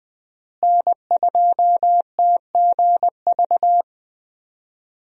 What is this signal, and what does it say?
Morse code, heard as a keyed tone, about 21 wpm, 700 Hz. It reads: N2TGV
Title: USA Call Signs - 5 Characters - Repeated 3x - 20wpm